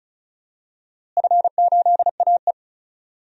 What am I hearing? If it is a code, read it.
Morse code, heard as a keyed tone, about 34 wpm, 700 Hz. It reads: F8AE
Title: DXCC-100 Call Signs - 4 Characters - Repeated 3x - 35wpm